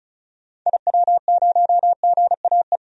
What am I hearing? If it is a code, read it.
Morse code, heard as a keyed tone, about 36 wpm, 700 Hz. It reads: IW0GAE
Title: DXCC-100 Call Signs - 6 Characters - 35wpm